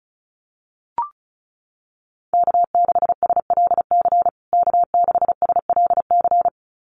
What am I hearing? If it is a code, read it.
Morse code, heard as a keyed tone, about 34 wpm, 700 Hz. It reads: K6SLC K6SLC
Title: USA Call Signs Repeated 2x 35wpm